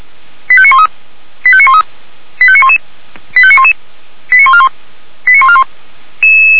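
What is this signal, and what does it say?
Five-tone selective call (ZVEI), heard as quick runs of five tones, tones of about 70 ms. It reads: 85913 85913 85911 85911 89141 89141
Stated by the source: ZVEI five-tone selective call received by SDR on channel 464.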